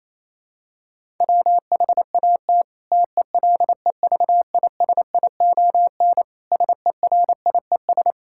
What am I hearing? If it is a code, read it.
Morse code, heard as a keyed tone, about 28 wpm, 700 Hz. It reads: WHAT TELEVISION SERIES
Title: Sentences from Top 1000 Words - Review Part 5 - 28wpm